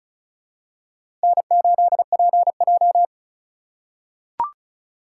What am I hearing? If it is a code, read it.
Morse code, heard as a keyed tone, about 34 wpm, 700 Hz. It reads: N8PJ
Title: USA Call Signs - 3x Character Spacing - 35wpm